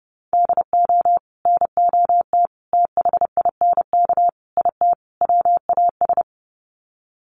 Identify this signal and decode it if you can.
Morse code, heard as a keyed tone, about 30 wpm, 700 Hz. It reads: DO NOT THINK IT WAS
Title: Sets of 5 Words Encore 30wpm